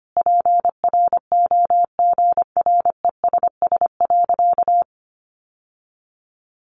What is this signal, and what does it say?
Morse code, heard as a keyed tone, about 25 wpm, 700 Hz. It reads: PROGRESS.
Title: Sentences from Top 1400 Words 25wpm